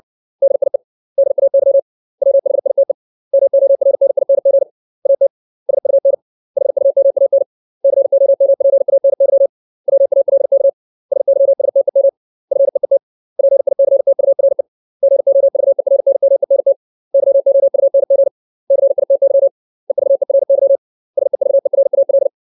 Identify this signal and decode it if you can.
Morse code, heard as a keyed tone, about 60 wpm, 550 Hz. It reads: DIE DAY WHITE MORNING AT SUN HUMAN COMPANY WALK SYSTEM FEET PICTURE GOVERNMENT YOUNG CITY EVERY SEVERAL
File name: 632 words / 632 Words 60WPM